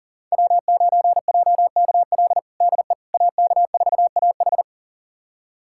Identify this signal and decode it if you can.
Morse code, heard as a keyed tone, about 37 wpm, 700 Hz. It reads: W9JKL DE AK4AH
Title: USA Callsign DE Callsign 40wpm